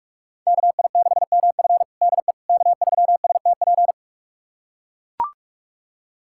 Morse code, heard as a keyed tone, about 44 wpm, 700 Hz. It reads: KI6MF DE K3STP
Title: USA Callsign DE Callsign - 5 Characters - 45wpm